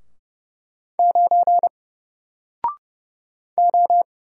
Morse code, heard as a keyed tone, about 30 wpm, 700 Hz. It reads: 9 O
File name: Single-Letter-Number-30wpm